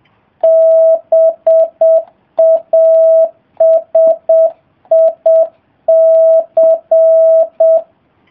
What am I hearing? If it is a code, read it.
Morse code, heard as a keyed tone, about 7 wpm, 650 Hz. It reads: BASIC